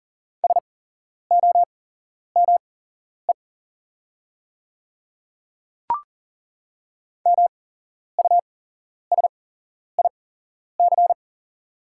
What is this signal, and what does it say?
Morse code, heard as a keyed tone, about 38 wpm, spaced as for 11 wpm, 700 Hz. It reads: SOME MUSIC
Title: 8x Character Spacing - Top 300 Words - 40wpm